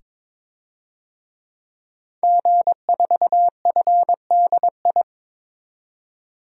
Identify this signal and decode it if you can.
Morse code, heard as a keyed tone, about 22 wpm, 700 Hz. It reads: G4FDI
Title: DXCC-100 Call Signs - 5 Characters - 22wpm